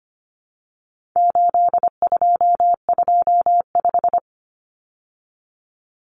Morse code, heard as a keyed tone, about 25 wpm, 700 Hz. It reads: 8225
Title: Numbers - 4-Digit 25wpm